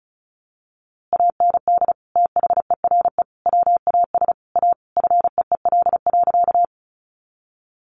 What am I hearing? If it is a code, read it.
Morse code, heard as a keyed tone, about 34 wpm, 700 Hz. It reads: AND THERE WAS A FEEL.
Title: A Sound Of Thunder 35wpm